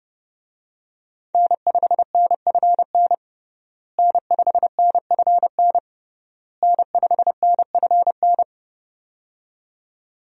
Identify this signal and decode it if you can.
Morse code, heard as a keyed tone, about 30 wpm, 700 Hz. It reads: N5NFN N5NFN N5NFN
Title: USA Call Signs - 5 Characters - Repeated 3x - 30wpm